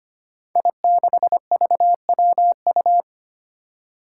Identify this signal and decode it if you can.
Morse code, heard as a keyed tone, about 25 wpm, 700 Hz. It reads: I6VWU
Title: Five Challenging Letter-Number Combinations 25wpm